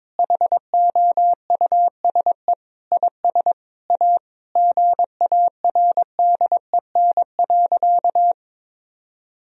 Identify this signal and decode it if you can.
Morse code, heard as a keyed tone, about 22 wpm, 700 Hz. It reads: HOUSE IS A GARDEN.
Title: Sentences from Top 1000 Words - Review Part 4 - 22wpm